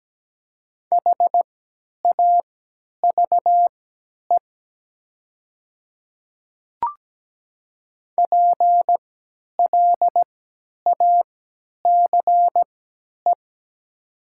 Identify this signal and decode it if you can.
Morse code, heard as a keyed tone, about 17 wpm, spaced as for 10 wpm, 700 Hz. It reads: HAVE PLACE
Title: 3x Character Spacing - Top 200 Words - 17wpm